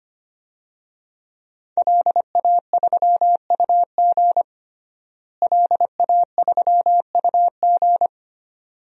Morse code, heard as a keyed tone, about 25 wpm, 700 Hz. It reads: LA3UG LA3UG
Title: DXCC-100 Call Signs - 5 Characters - Repeated 2x - 25wpm